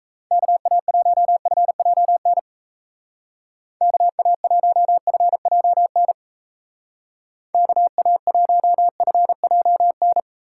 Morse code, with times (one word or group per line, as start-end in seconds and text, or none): 0.3-2.4 KA1FJN
3.8-6.1 KA1FJN
7.5-10.2 KA1FJN